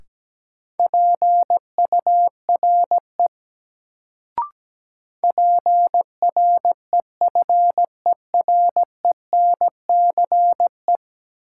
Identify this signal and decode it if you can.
Morse code, heard as a keyed tone, about 17 wpm, 700 Hz. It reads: PURE PREFERENCE
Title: Sentences from Top 1500 Words - New Word Drill Alt - 17wpm